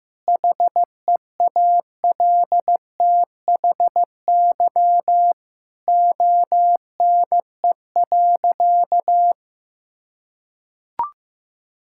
Morse code, heard as a keyed tone, about 15 wpm, 700 Hz. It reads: HEALTHY ONE.
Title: Sentences from Top 1000 Words - Review Part 5 - 15wpm